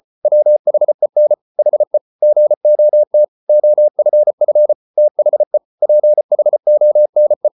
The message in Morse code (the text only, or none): WHEN HE GOT OFF THE PHONE